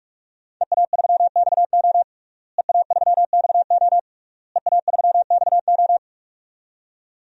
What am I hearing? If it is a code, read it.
Morse code, heard as a keyed tone, about 46 wpm, 700 Hz. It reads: EA3XO EA3XO EA3XO
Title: DXCC-100 Call Signs - 5 Characters - Repeated 3x - 45wpm